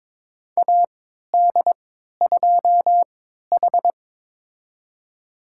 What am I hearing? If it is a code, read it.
Morse code, heard as a keyed tone, about 22 wpm, spaced as for 13 wpm, 700 Hz. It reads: AD2H
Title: USA Call Signs - 4 Characters - 3x Character Spacing - 22wpm